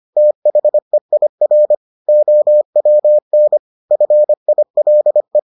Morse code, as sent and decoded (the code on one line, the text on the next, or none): - .... .. .-. / --- .-- -. / ..-. .. .-.. .
THEIR OWN FILE